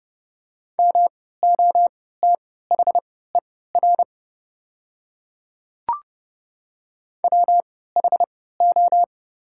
Morse code, heard as a keyed tone, about 29 wpm, spaced as for 17 wpm, 700 Hz. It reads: MOTHER WHO
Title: 3x Character Spacing - Top 200 Words - 30wpm